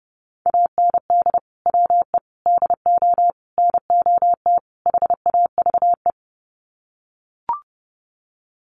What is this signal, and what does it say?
Morse code, heard as a keyed tone, about 30 wpm, 700 Hz. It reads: AND WE DO NOT HAVE
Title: Sets of 5 Words Encore 30wpm